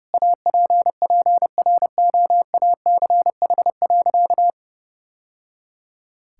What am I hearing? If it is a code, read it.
Morse code, heard as a keyed tone, about 32 wpm, 700 Hz. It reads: APPROACH.
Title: Sentences from Top 1000 Words - Review Part 6 - 30wpm